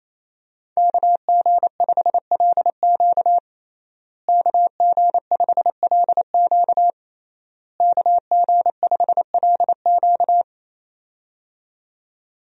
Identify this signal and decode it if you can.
Morse code, heard as a keyed tone, about 28 wpm, 700 Hz. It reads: KG5LQ KG5LQ KG5LQ
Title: USA Call Signs - 5 Characters - Repeated 3x - 28wpm